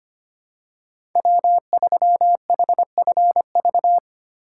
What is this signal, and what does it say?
Morse code, heard as a keyed tone, about 25 wpm, 700 Hz. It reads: W3HFV
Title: USA Call Signs - 5 Characters - 4x Character Spacing - 25wpm